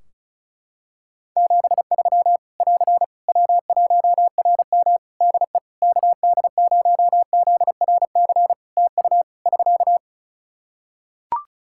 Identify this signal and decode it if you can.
Morse code, heard as a keyed tone, about 36 wpm, 700 Hz. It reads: 73 <AR> W1RM DE KD0ZRC TU <SK>
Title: Practice QSO - Exchange 4 - Ending - 35wpm